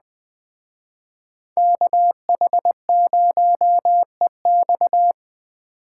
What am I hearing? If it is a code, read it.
Morse code, heard as a keyed tone, about 20 wpm, 700 Hz. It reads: KH0EX